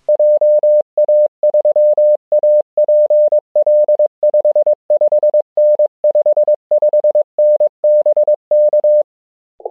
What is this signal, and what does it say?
Morse code, heard as a keyed tone, about 22 wpm, 600 Hz. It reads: JA3APL 55N 55N BK